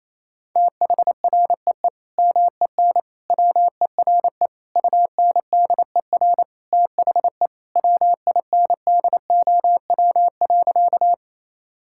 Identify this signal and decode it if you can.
Morse code, heard as a keyed tone, about 28 wpm, 700 Hz. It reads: THREE MEN WERE UNDER THE WINDOW.
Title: Sentences from Top 1600 Words 28wpm